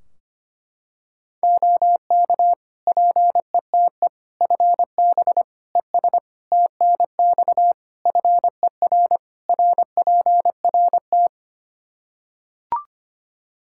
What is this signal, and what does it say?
Morse code, heard as a keyed tone, about 25 wpm, 700 Hz. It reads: OK PETE FB ES TNX FER RPRT
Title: Practice QSO - Exchange 2 - Info - 25wpm